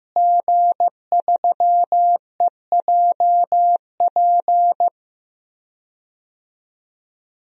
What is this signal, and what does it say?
Morse code, heard as a keyed tone, about 16 wpm, 700 Hz. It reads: G3EJP
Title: DXCC-100 Call Signs - 5 Characters - Repeated 3x - 15wpm